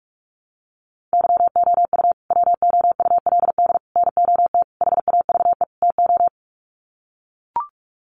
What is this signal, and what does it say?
Morse code, heard as a keyed tone, about 44 wpm, 700 Hz. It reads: YOU WOULD NOT HAVE TO
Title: Sets of 5 Words Encore 45wpm